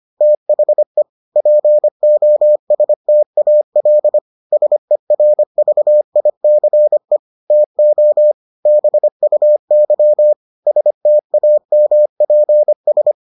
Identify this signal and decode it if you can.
Morse code, heard as a keyed tone, about 25 wpm, 600 Hz. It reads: THE POSTAL SERVICE TO BUY STAMPS